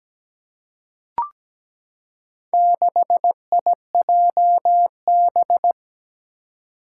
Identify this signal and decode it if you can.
Morse code, heard as a keyed tone, about 17 wpm, 700 Hz. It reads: 6IJB
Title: Four Challenging Letter-Number Combinations 17wpm